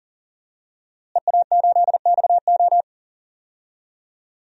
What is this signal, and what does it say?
Morse code, heard as a keyed tone, about 37 wpm, 700 Hz. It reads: EA8XO
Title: DXCC-100 Call Signs - 5 Characters - Repeated 3x - 40wpm